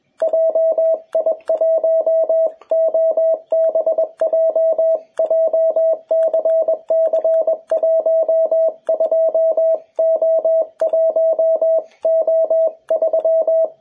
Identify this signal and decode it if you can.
Morse code, heard as a keyed tone, about 21 wpm, 650 Hz, with radio noise. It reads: JI1OBJ JCC12O1O3
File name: jh8cbh / 2021cw